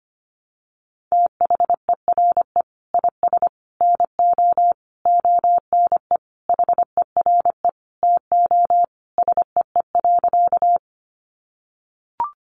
Morse code, heard as a keyed tone, about 25 wpm, 700 Hz. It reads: THERE IS NO ONE HERE TO SEE.
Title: Sentences from Top 200 Words 25wpm